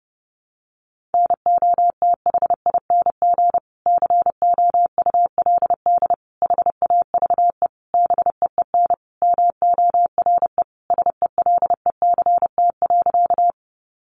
Morse code, heard as a keyed tone, about 30 wpm, 700 Hz. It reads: NOTHING COULD HAVE BEEN MORE SELECT.
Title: Sentences from Top 1300 Words 30wpm